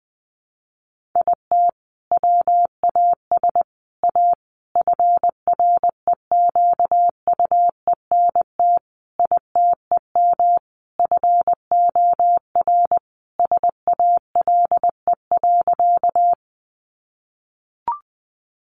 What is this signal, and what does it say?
Morse code, heard as a keyed tone, about 19 wpm, 700 Hz. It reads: IT WAS A FREQUENT ITEM FOR SALE.
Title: Sentences from Top 1400 Words 20wpm